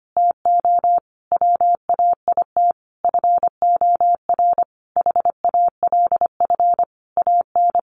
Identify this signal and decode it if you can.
Morse code, heard as a keyed tone, about 25 wpm, 700 Hz. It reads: TO WAIT FOR HALF AN
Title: Sentences from Top 1100 Words 25wpm